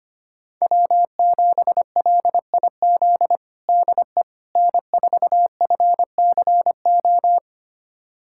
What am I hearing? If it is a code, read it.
Morse code, heard as a keyed tone, about 25 wpm, 700 Hz. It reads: W7LIZ DE N4FCO
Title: USA Callsign DE Callsign - 5 Characters - 25wpm